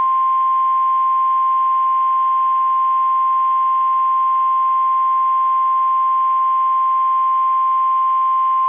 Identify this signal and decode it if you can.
Emergency Alert System (SAME) tones (the 1050 Hz attention tone). It is heard from 1.1 to 8.0 s.
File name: eas / KSDZKDJL / KSDZKDJL